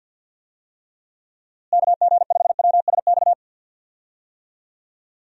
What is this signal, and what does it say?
Morse code, heard as a keyed tone, about 52 wpm, 700 Hz. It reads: KG5WSX
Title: USA Call Signs - 6 Characters - Repeated 3x - 50wpm